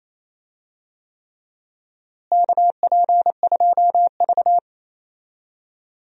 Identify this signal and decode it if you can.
Morse code, heard as a keyed tone, about 28 wpm, 700 Hz. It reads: KP2V
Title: DXCC-100 Call Signs - 4 Characters - Repeated 3x - 28wpm